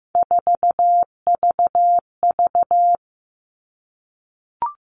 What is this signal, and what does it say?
Morse code, heard as a keyed tone, about 15 wpm, 700 Hz. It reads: 4VV
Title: All Cut Numbers - 3-Digit 15wpm